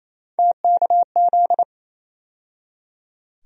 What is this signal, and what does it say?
Morse code, heard as a keyed tone, about 27 wpm, 700 Hz. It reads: TKZ
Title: USA Call Sign Postfixes - 3 Letters 28wpm